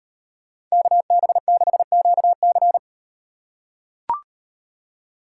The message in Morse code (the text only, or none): KB6QC